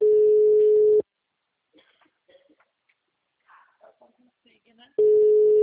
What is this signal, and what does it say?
Ringback tone (Europe, 425 Hz), with speech.